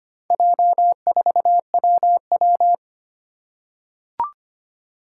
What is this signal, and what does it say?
Morse code, heard as a keyed tone, about 25 wpm, 700 Hz. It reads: J4WW
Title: Four Challenging Letter-Number Combinations 25wpm